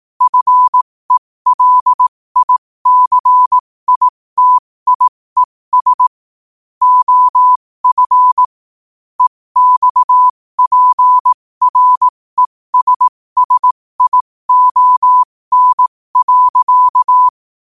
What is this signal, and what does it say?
Morse code, heard as a keyed tone, about 18 wpm, spaced as for 15 wpm, 1000 Hz. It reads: FELICITIES OF EXPRESSION.
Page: New quotes every day in morse code at 15 Words per minute.